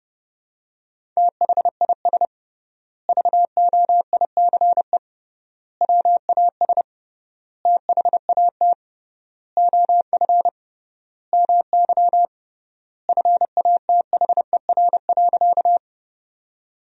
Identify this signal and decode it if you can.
Morse code, heard as a keyed tone, about 32 wpm, 700 Hz. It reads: THIS VOICE WAS THAT OF MY FATHER.
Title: 3x Word-Spacing - Sentences from Top 500 Words - 30wpm